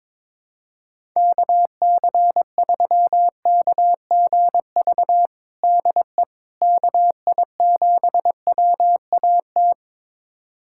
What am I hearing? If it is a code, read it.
Morse code, heard as a keyed tone, about 22 wpm, 700 Hz. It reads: KC3KGV DE KI7WAT
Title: USA Callsign DE Callsign 22wpm